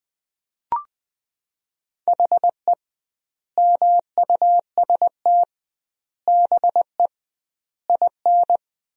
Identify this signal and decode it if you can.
Morse code, heard as a keyed tone, about 20 wpm, 700 Hz. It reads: HE MUST BE IN
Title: 2x Word-Spacing - Sentences from Top 500 Words - 20wpm